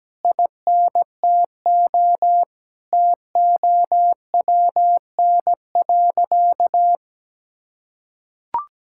Morse code, heard as a keyed tone, about 17 wpm, 700 Hz. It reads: INTO TOWN.
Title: Sentences from Top 1600 Words 17wpm